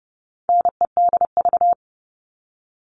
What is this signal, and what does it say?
Morse code, heard as a keyed tone, about 30 wpm, 700 Hz. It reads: NEDV
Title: All Cut Numbers - 4-Digit 30wpm